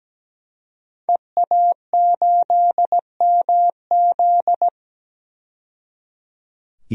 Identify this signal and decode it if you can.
Morse code, heard as a keyed tone, about 17 wpm, 700 Hz. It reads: EA8MZ
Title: DXCC-100 Call Signs - 5 Characters - 17wpm